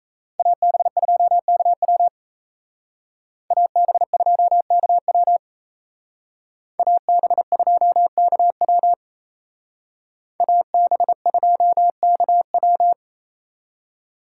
Morse code, heard as a keyed tone, about 34 wpm, 700 Hz. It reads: AB2KW AB2KW AB2KW AB2KW
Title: Speed Racing - USA 2x2 Call Signs 28wpm